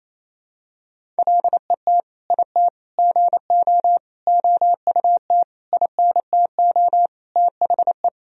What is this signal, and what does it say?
Morse code, heard as a keyed tone, about 28 wpm, 700 Hz. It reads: LET IT GO OUT INTO THE
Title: Sentences from Top 1000 Words - Review Part 1 - 28wpm